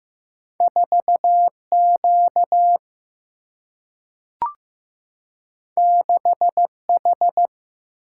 Morse code, heard as a keyed tone, about 15 wpm, 700 Hz. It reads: 4 Q 6 H
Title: Single-Letter Single-Number 15wpm